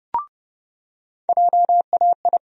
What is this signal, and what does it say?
Morse code, heard as a keyed tone, about 30 wpm, 700 Hz. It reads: JAI
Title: USA Call Sign Postfixes - 3 Letters 30wpm